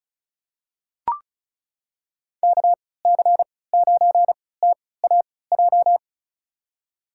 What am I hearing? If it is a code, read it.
Morse code, heard as a keyed tone, about 36 wpm, spaced as for 20 wpm, 700 Hz. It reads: KC9TAJ
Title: USA Call Signs - 6 Characters - 3x Character Spacing - 35wpm